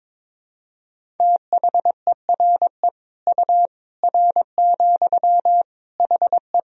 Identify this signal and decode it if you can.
Morse code, heard as a keyed tone, about 21 wpm, 700 Hz. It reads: THERE U R, HE
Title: Sentences from Top 100 Words and Abbreviations 22wpm